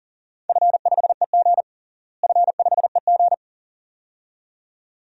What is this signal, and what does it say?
Morse code, heard as a keyed tone, about 40 wpm, 700 Hz. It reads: F5EG F5EG
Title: DXCC-100 Call Signs - 4 Characters - Repeated 2x - 40wpm